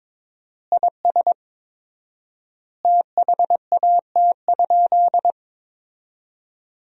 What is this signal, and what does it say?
Morse code, heard as a keyed tone, about 22 wpm, 700 Hz. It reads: IS THAT?
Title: 4x Word-Spacing - Sentences from Top 100 Words - 22wpm